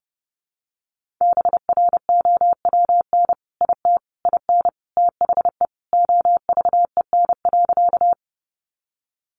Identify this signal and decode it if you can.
Morse code, heard as a keyed tone, about 30 wpm, 700 Hz. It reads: BROWN IT IN THE OVEN.
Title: Sentences from Top 1200 Words 30wpm